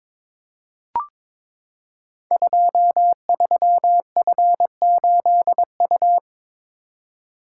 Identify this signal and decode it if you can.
Morse code, heard as a keyed tone, about 22 wpm, 700 Hz. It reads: 23F8U
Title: Code Groups - 5 Letter:Numbers - 22wpm